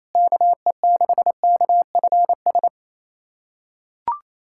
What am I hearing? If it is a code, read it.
Morse code, heard as a keyed tone, about 28 wpm, 700 Hz. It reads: KE6KFS
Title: USA Callsign DE Callsign - 6 Characters - 28wpm